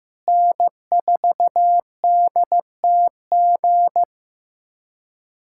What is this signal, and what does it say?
Morse code, heard as a keyed tone, about 15 wpm, 700 Hz. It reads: N4DTG
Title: USA Callsign DE Callsign 15wpm